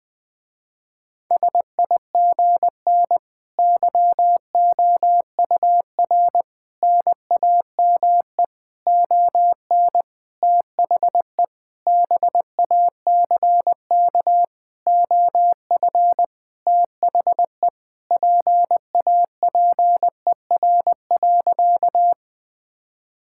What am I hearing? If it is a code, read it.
Morse code, heard as a keyed tone, about 20 wpm, 700 Hz. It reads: SIGN YOUR NAME ON THE BACK OF THE PAPER.
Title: Sentences with 5-Word Phrases - Spoken Beginning - 20wpm